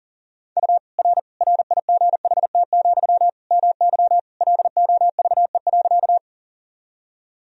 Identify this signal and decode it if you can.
Morse code, heard as a keyed tone, about 40 wpm, 700 Hz. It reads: U R RIGHT, MY LOVE.
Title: Sentences from Top 400 Words and Abbreviations 40wpm